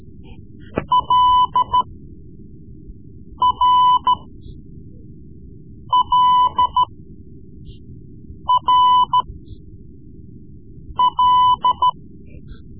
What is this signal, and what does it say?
Morse code, heard as a keyed tone, about 13 wpm, spaced as for 4 wpm, 1000 Hz. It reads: LRLRL
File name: Morse Lessons / Morse Lessons / Tape-3B